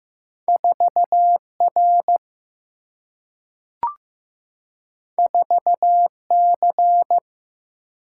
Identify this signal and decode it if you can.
Morse code, heard as a keyed tone, about 15 wpm, 700 Hz. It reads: 4R 4C
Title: Single-Letter Single-Number 15wpm